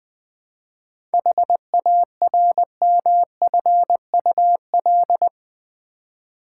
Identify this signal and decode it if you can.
Morse code, heard as a keyed tone, about 20 wpm, 700 Hz. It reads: HARMFUL
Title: 7 Letter Words - First 3 Letters Given - 20wpm